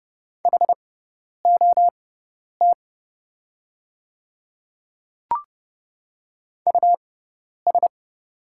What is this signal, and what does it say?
Morse code, heard as a keyed tone, about 30 wpm, spaced as for 10 wpm, 690 Hz. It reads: HOT US
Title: 6x Character Spacing - Top 200 Words - 30wpm